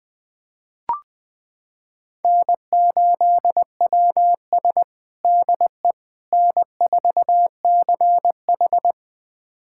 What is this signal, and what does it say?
Morse code, heard as a keyed tone, about 21 wpm, 700 Hz. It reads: N8WS DE N4CH
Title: USA Callsign DE Callsign - 4 Characters - 20wpm